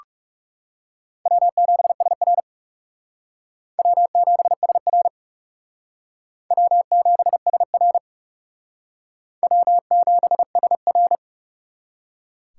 Morse code, heard as a keyed tone, about 37 wpm, 700 Hz. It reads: W7SR W7SR W7SR W7SR
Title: Speed Racing - USA 1x2 Call Signs 30wpm